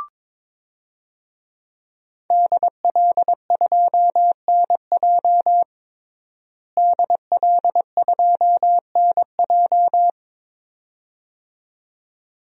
Morse code, heard as a keyed tone, about 22 wpm, 700 Hz. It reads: DL2NJ DL2NJ
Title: DXCC-100 Call Signs - 5 Characters - Repeated 2x - 22wpm